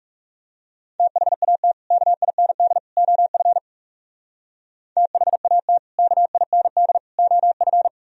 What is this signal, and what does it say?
Morse code, heard as a keyed tone, about 36 wpm, 700 Hz. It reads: THAT KIND OF THAT KIND OF
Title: Speed-Racing - Sets of 3 Words 30wpm